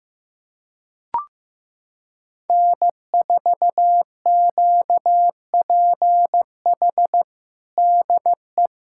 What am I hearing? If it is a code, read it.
Morse code, heard as a keyed tone, about 15 wpm, 700 Hz. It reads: N4QPH DE
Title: USA Callsign DE Callsign 15wpm